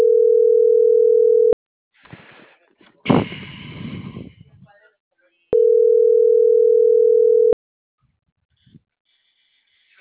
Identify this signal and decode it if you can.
Ringback tone (North America), 440 + 480 Hz.